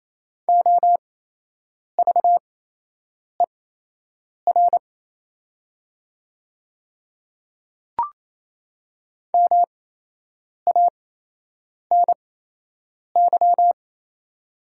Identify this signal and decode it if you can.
Morse code, heard as a keyed tone, about 28 wpm, spaced as for 8 wpm, 700 Hz. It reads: OVER MANY
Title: 8x Character Spacing - Top 100 Words - 28wpm